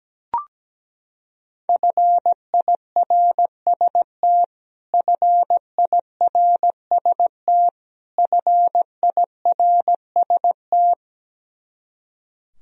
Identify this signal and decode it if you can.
Morse code, heard as a keyed tone, about 17 wpm, 700 Hz. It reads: FIRST FIRST FIRST
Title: Top 100 Words Repeated 17wpm